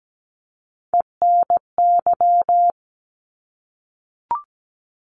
Morse code, heard as a keyed tone, about 17 wpm, 700 Hz. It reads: ENY
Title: ARRL Sections 17wpm